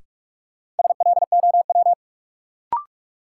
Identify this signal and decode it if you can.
Morse code, heard as a keyed tone, about 46 wpm, 700 Hz. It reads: SLOW
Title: Sentences from Top 400 Words - New Word Drill - 45wpm